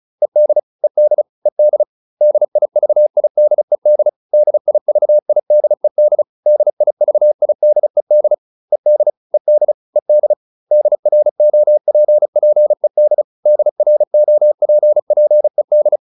Morse code, heard as a keyed tone, about 34 wpm, 600 Hz. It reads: ED ED ED DIVIDED DIVIDED DIVIDED ED ED ED DROPPED DROPPED